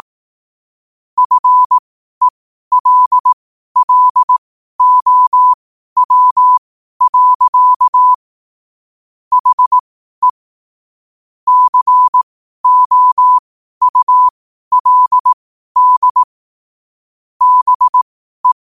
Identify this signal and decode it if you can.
Morse code, heard as a keyed tone, about 18 wpm, spaced as for 12 wpm, 1000 Hz. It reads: FELLOW. HE COULD BE
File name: qotdfiles / 12wpm